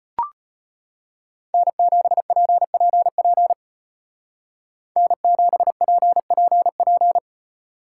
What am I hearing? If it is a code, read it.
Morse code, heard as a keyed tone, about 33 wpm, 700 Hz. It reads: N7PPP N7PPP